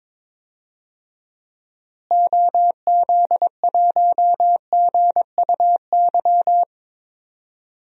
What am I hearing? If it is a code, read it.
Morse code, heard as a keyed tone, about 22 wpm, 700 Hz. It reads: OZ1GUY
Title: DXCC-100 Call Signs - 6 Characters - Repeated 3x - 22wpm